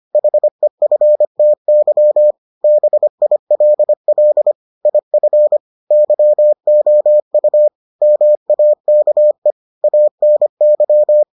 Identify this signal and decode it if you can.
Morse code, heard as a keyed tone, about 25 wpm, 600 Hz. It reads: HEFTY BILL IF YOU MAKE ANY